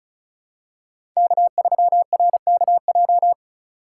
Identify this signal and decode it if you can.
Morse code, heard as a keyed tone, about 36 wpm, 700 Hz. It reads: K3RKJ
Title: USA Call Signs - 5 Characters - 2x Character Spacing - 35wpm